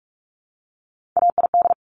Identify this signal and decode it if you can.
Morse code, heard as a keyed tone, about 46 wpm, 700 Hz. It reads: AID
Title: Top Three Letter Combinations in Words 45wpm